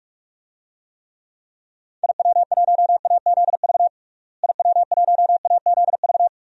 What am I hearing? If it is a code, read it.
Morse code, heard as a keyed tone, about 46 wpm, 700 Hz. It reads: IW1AZV IW1AZV
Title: DXCC-100 Call Signs - 6 Characters - Repeated 3x - 45wpm